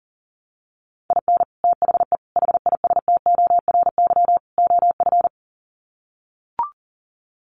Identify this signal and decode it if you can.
Morse code, heard as a keyed tone, about 40 wpm, 700 Hz. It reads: IN THE HISTORY OF
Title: Sets of 4 Words 40wpm